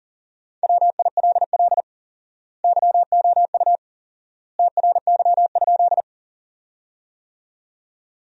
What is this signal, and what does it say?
Morse code, heard as a keyed tone, about 40 wpm, 700 Hz. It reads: WILL YOU TRY?
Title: 4x Word-Spacing - Sentences from Top 200 Words - 40wpm